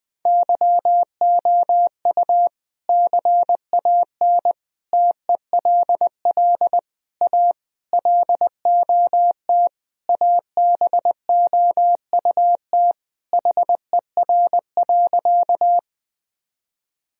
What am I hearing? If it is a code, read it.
Morse code, heard as a keyed tone, about 21 wpm, 700 Hz. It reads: YOU CAN TELL A LOT ABOUT HER.
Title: Sentences from Top 1000 Words - Review Part 3 - 20wpm